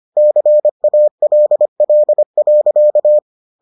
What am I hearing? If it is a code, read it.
Morse code, heard as a keyed tone, about 25 wpm, 600 Hz. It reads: CALL.